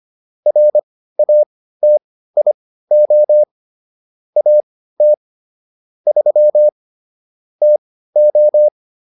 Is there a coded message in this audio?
Yes: Morse code from 0.5 to 8.7 s, heard as a keyed tone, about 25 wpm, spaced as for 15 wpm, 600 Hz. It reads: RATIO AT 3 TO